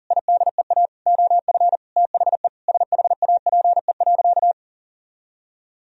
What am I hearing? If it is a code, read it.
Morse code, heard as a keyed tone, about 40 wpm, 700 Hz. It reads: IDEA OF THE SHAPE.
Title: Sentences from Top 1000 Words - Review Part 3 - 40wpm